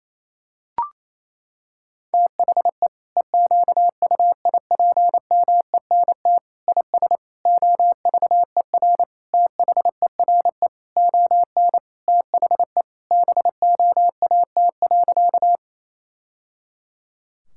Morse code, heard as a keyed tone, about 28 wpm, 700 Hz. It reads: THE EQUIPMENT IS OVER THERE ON THE BOAT.
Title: Sentences from Top 1000 Words - Review Part 5 - 28wpm